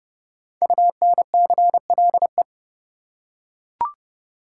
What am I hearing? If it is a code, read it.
Morse code, heard as a keyed tone, about 30 wpm, 700 Hz. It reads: UNCLE
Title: Five Letter Words - 30wpm